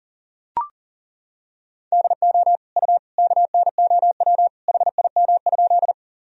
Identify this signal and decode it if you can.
Morse code, heard as a keyed tone, about 40 wpm, 700 Hz. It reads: DO U KNOW HIM?
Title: Sentences from Top 100 Words and Abbreviations 40wpm